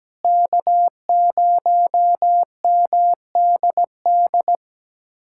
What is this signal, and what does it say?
Morse code, heard as a keyed tone, about 17 wpm, 700 Hz. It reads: K0MDD
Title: USA Callsign DE Callsign 17wpm